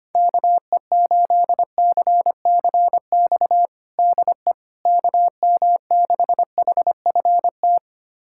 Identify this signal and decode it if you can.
Morse code, heard as a keyed tone, about 25 wpm, 700 Hz. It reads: KE8CCX DE KM6HFT
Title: USA Callsign DE Callsign - 6 Characters - 25wpm